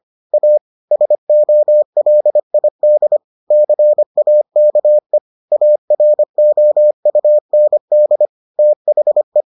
Morse code, heard as a keyed tone, about 25 wpm, 600 Hz. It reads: A SOLID CAKE AROUND THE